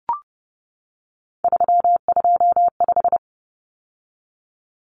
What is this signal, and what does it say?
Morse code, heard as a keyed tone, about 30 wpm, 700 Hz. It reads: 325
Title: Numbers - 3-Digit 30wpm